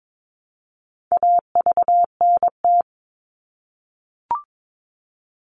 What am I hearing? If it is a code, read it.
Morse code, heard as a keyed tone, about 22 wpm, 700 Hz. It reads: AVNT
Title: All Cut Numbers - 4-Digit 22wpm